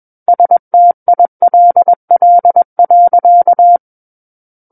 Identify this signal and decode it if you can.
Morse code, heard as a keyed tone, about 21 wpm, 700 Hz. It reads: STILL.
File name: Book One / 12-Chapter 12 - 21 WPM